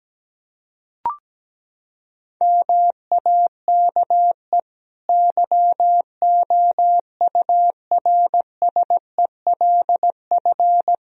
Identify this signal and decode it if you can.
Morse code, heard as a keyed tone, about 17 wpm, 700 Hz. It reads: MAKE YOURSELF
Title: Sentences from Top 1000 Words - Review Part 3 - 17wpm